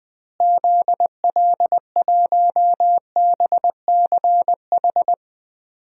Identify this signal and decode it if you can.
Morse code, heard as a keyed tone, about 21 wpm, 700 Hz. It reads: ZL1BCH
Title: DXCC-100 Call Signs - 6 Characters - Repeated 2x - 20wpm